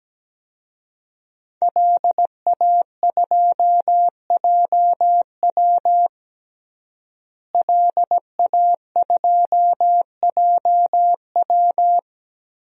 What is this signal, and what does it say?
Morse code, heard as a keyed tone, about 17 wpm, 700 Hz. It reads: LA2JW LA2JW
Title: DXCC-100 Call Signs - 5 Characters - Repeated 2x - 17wpm